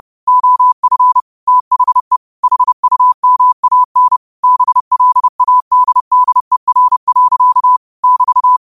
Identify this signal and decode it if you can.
Morse code, heard as a keyed tone, about 30 wpm, 1000 Hz. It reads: OF THE HUMAN BLADDER. <BT>
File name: qotdfiles / 30wpm